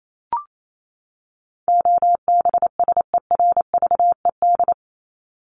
Morse code, heard as a keyed tone, about 28 wpm, 700 Hz. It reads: OBSERVED
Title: ED Suffix - Top 100 Words with Base Words 28wpm